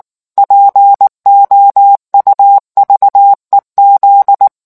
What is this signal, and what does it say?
Morse code, heard as a keyed tone, about 19 wpm, 800 Hz. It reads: POUVEZ